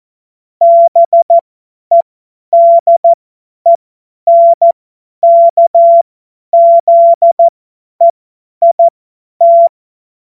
Morse code, heard as a keyed tone, about 14 wpm, spaced as for 10 wpm, 680 Hz. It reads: BEDENKZEIT
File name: Morsekurs / Witze / Witz 20. Dezember 2022 50 F70